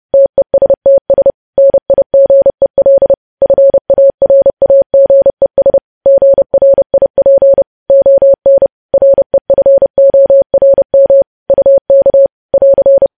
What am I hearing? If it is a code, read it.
Morse code, heard as a keyed tone, about 30 wpm, 550 Hz. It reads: TESTS NIGEL FARAGES GRIP ON REFORM UK <AR>
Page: news headlines in morse code at 30 wpm